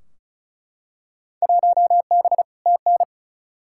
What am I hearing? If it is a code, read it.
Morse code, heard as a keyed tone, about 36 wpm, 700 Hz. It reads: 1B TN
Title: Field Day Exchanges - 35wpm